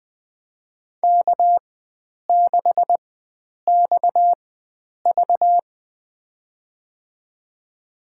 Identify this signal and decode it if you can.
Morse code, heard as a keyed tone, about 20 wpm, spaced as for 9 wpm, 700 Hz. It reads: K6XV
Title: USA Call Signs - 4 Characters - 4x Character Spacing - 20wpm